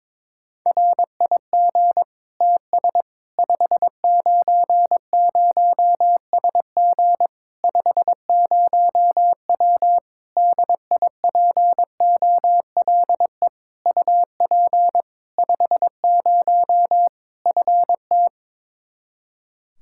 Morse code, heard as a keyed tone, about 22 wpm, 700 Hz. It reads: RIG TS 590SG 50W DIPOLE UP 50 FT
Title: Practice QSO - Exchange 2 - Info - 22wpm